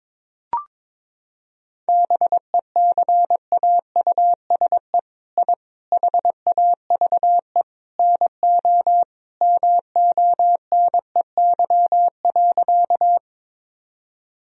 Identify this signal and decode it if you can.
Morse code, heard as a keyed tone, about 21 wpm, 700 Hz. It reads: BECAUSE I HAVE NO MONEY.
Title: Sentences from Top 1000 Words - Review Part 3 - 22wpm